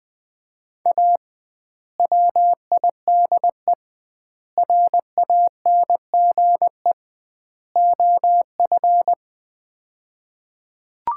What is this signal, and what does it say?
Morse code, heard as a keyed tone, about 20 wpm, 700 Hz. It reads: A WIDE RANGE OF
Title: 2x Word-Spacing - Sets of 4 Words - 20wpm